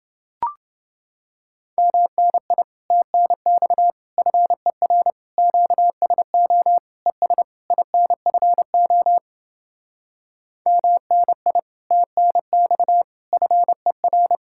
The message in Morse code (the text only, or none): MNI TNX FER QSO ES INFO MNI TNX FER